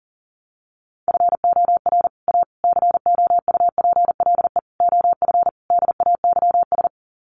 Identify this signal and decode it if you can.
Morse code, heard as a keyed tone, about 40 wpm, 700 Hz. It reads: FOR A COUPLE OF DAYS
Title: Sets of 5 Words Encore2 40wpm